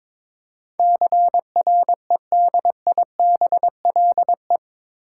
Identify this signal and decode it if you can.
Morse code, heard as a keyed tone, about 22 wpm, 700 Hz. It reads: CREDIBLE
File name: practice / 22wpm / Eight Letter Words - 22wpm